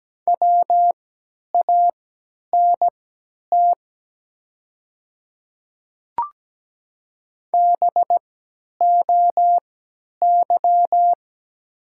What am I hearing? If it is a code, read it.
Morse code, heard as a keyed tone, about 17 wpm, spaced as for 10 wpm, 700 Hz. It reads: WANT BOY
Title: 3x Character Spacing - Top 300 Words - 17wpm